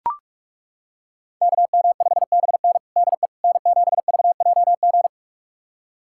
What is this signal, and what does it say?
Morse code, heard as a keyed tone, about 42 wpm, 700 Hz. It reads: KM5BN DE N7VJG
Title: USA Callsign DE Callsign - 5 Characters - 45wpm